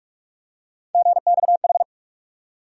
Morse code, heard as a keyed tone, about 46 wpm, 700 Hz. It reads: GXH
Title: USA Call Sign Postfixes - 3 Letters 45wpm